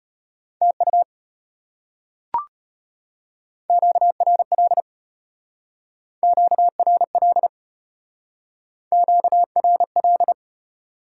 Morse code, heard as a keyed tone, about 32 wpm, 690 Hz. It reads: TU QRL QRL QRL